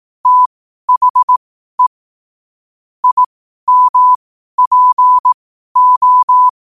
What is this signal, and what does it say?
Morse code, heard as a keyed tone, about 18 wpm, spaced as for 12 wpm, 1000 Hz. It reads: THE IMPO
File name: qotdfiles / 12wpm